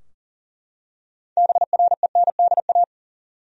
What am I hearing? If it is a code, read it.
Morse code, heard as a keyed tone, about 40 wpm, 700 Hz. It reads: BRENDA
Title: USA Common Names - 40wpm